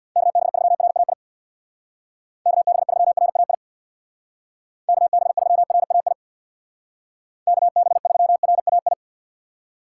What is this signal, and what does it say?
Morse code, heard as a keyed tone, about 65 wpm, 700 Hz. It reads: KB3RAI KB3RAI KB3RAI KB3RAI